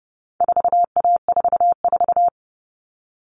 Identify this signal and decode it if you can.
Morse code, heard as a keyed tone, about 30 wpm, 700 Hz. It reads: 4A44
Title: All Cut Numbers - 4-Digit 30wpm